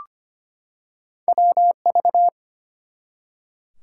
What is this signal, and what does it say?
Morse code, heard as a keyed tone, about 25 wpm, 700 Hz. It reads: WV
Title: US States and Canadian Provinces 25wpm